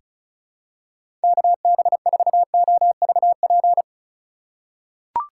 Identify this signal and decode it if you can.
Morse code, heard as a keyed tone, about 36 wpm, 700 Hz. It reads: KB4OVP
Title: USA Call Signs - 6 Characters - 2x Character Spacing - 35wpm